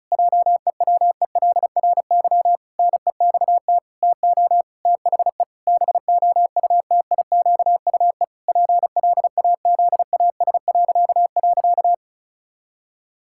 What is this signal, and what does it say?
Morse code, heard as a keyed tone, about 36 wpm, 700 Hz. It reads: JEWELRY NEXT TO THE BOUTIQUE PLAZAS..
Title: Pangram Challenge v2 - 35wpm